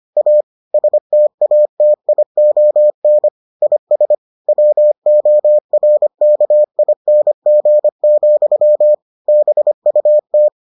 Morse code, heard as a keyed tone, about 25 wpm, 600 Hz. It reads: A STATION IS WORKING, BUT